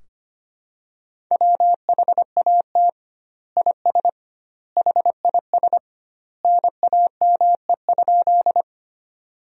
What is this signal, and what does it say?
Morse code, heard as a keyed tone, about 25 wpm, 700 Hz. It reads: WHAT IS HIS NAME?
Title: 2x Word-Spacing - Sentences from Top 200 Words - 25wpm